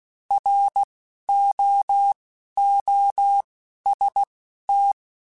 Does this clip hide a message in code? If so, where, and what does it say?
Yes: Morse code from 0.3 to 4.9 s, heard as a keyed tone, about 16 wpm, 800 Hz. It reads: R O O S T